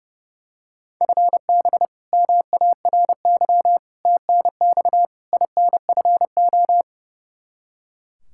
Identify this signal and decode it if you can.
Morse code, heard as a keyed tone, about 30 wpm, 700 Hz. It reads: FB MARY TNX INFO
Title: Practice QSO - Exchange 3 - More Info - 30wpm